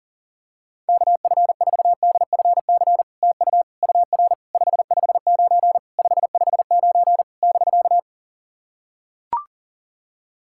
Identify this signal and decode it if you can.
Morse code, heard as a keyed tone, about 40 wpm, 700 Hz. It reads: KF4DFC TU UR 559 559 <BK>
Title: SOTA QSO Practice 40wpm